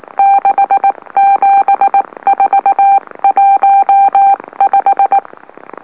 Morse code, heard as a keyed tone, about 19 wpm, spaced as for 16 wpm, 790 Hz. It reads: 67415
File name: M04- -2345utc-LOLO